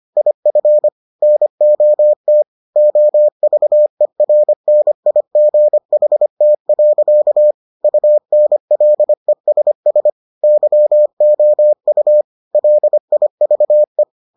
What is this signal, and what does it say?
Morse code, heard as a keyed tone, about 25 wpm, 600 Hz. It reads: IF NOT OVERNIGHT. UNLESS YOU LIVE